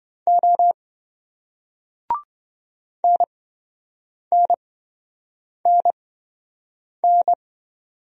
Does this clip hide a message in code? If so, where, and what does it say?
Yes: Morse code from 0.3 to 7.3 s, heard as a keyed tone, about 28 wpm, spaced as for 7 wpm, 700 Hz. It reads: O NNNN